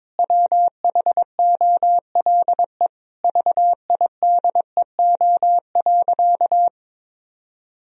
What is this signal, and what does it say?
Morse code, heard as a keyed tone, about 21 wpm, 700 Hz. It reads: WHOLE VIDEO.
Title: Sentences from Top 1000 Words - Review Part 4 - 22wpm